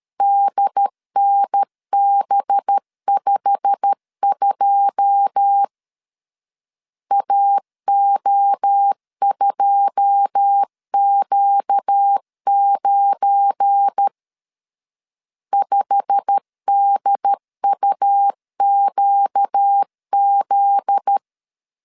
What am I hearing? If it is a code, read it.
Morse code, heard as a keyed tone, about 13 wpm, 800 Hz. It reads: DNB52 AO2Q9 5DUQZ
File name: CW / DET / 1 9 1500C S C 12wpm 3811